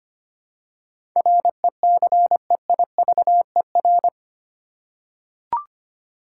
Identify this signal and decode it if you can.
Morse code, heard as a keyed tone, about 25 wpm, 700 Hz. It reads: RECEIVER
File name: 8 Letter Words - First 3 Letters Given - 25wpm